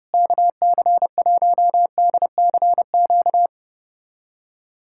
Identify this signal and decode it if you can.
Morse code, heard as a keyed tone, about 32 wpm, 700 Hz. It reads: KC1DCQ
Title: USA Callsign DE Callsign 30wpm